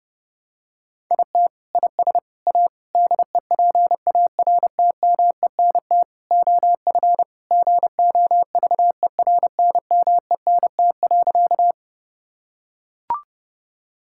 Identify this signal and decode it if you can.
Morse code, heard as a keyed tone, about 30 wpm, 700 Hz. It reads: IT IS A DEPARTMENT OF GOVERNMENT.
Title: Sentences from Top 1000 Words - Review Part 5 - 30wpm